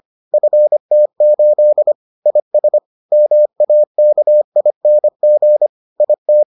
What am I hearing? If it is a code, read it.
Morse code, heard as a keyed tone, about 25 wpm, 600 Hz. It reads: FT8 IS MAKING IT